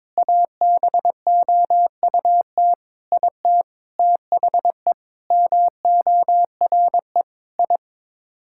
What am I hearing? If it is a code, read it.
Morse code, heard as a keyed tone, about 22 wpm, 700 Hz. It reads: ABOUT IT THE MORE I
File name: Sets of 9 Words - 22wpm